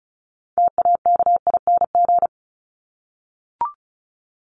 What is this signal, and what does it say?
Morse code, heard as a keyed tone, about 34 wpm, 700 Hz. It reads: TAKING
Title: ING Suffix - Top 100 Words with Base Words 35wpm